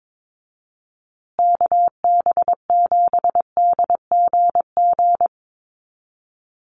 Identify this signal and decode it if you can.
Morse code, heard as a keyed tone, about 22 wpm, 700 Hz. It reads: KB7DGG
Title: USA Call Signs Repeated 2x 22wpm